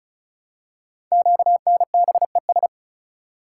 Morse code, heard as a keyed tone, about 35 wpm, 700 Hz. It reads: QNBES
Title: Code Groups - 5 Letters - 35wpm